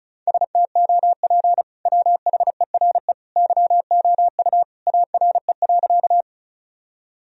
Morse code, heard as a keyed tone, about 34 wpm, 700 Hz. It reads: STOP WHERE YOU ARE.
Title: Sentences from Top 1000 Words - Review Part 1 - 35wpm